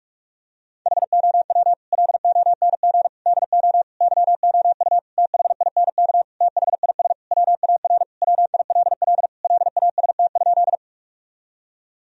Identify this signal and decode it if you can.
Morse code, heard as a keyed tone, about 47 wpm, 700 Hz. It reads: HOW LONG DO YOU THINK THIS WAR WILL LAST?
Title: Sentences from Top 1000 Words - Review Part 2 - 45wpm